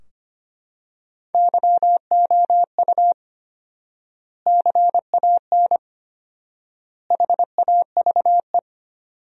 Morse code, heard as a keyed tone, about 25 wpm, 700 Hz. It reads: YOU CAN HAVE
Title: 4x Word-Spacing - Sentences from Top 100 Words - 25wpm